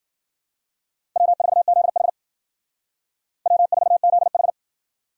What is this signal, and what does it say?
Morse code, heard as a keyed tone, about 51 wpm, 700 Hz. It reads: W4ZH W4ZH